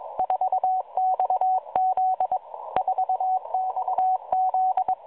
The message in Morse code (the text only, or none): 4XZ 4XZ